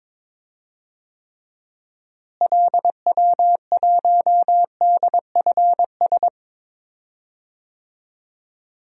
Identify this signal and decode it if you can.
Morse code, heard as a keyed tone, about 22 wpm, 700 Hz. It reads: LW1DFS
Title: DXCC-100 Call Signs - 6 Characters - Repeated 2x - 22wpm